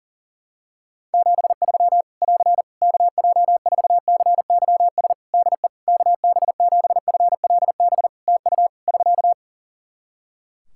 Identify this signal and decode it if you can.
Morse code, heard as a keyed tone, about 40 wpm, 700 Hz. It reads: 73 <AR> KJ4CYS DE KB7FLB TU <SK>
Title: Practice QSO - Exchange 4 - Ending - 40wpm